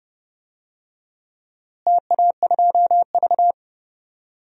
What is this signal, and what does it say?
Morse code, heard as a keyed tone, about 30 wpm, 700 Hz. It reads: TA2V
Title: DXCC-100 Call Signs - 4 Characters - Repeated 3x - 30wpm